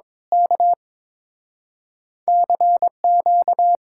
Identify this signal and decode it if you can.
Morse code, heard as a keyed tone, about 26 wpm, 700 Hz. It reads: K CQ
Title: Speed-Racing Common QSO Phrases Encore - 22wpm